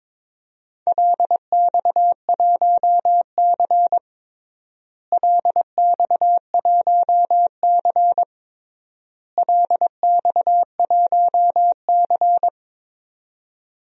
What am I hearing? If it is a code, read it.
Morse code, heard as a keyed tone, about 22 wpm, 700 Hz. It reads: LX1C LX1C LX1C
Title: DXCC-100 Call Signs - 4 Characters - Repeated 3x - 22wpm